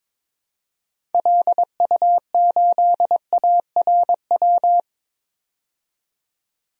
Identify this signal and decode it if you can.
Morse code, heard as a keyed tone, about 23 wpm, 700 Hz. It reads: LU8ARW
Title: DXCC-100 Call Signs - 6 Characters - 22wpm